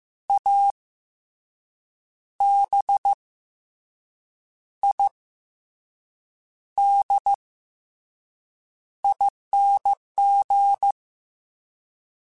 Morse code, heard as a keyed tone, about 15 wpm, 800 Hz. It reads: A B I D ING